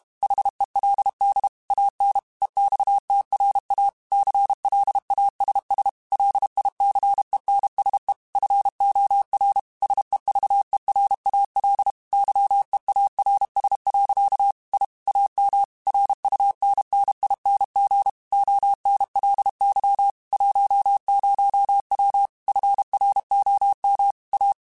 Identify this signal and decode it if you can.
Morse code, heard as a keyed tone, about 32 wpm, 800 Hz. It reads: HELD AN EXTRA CLASS LICENSE FOR SEVERAL YEARS. I AM RUNNING ONLY 10W FROM A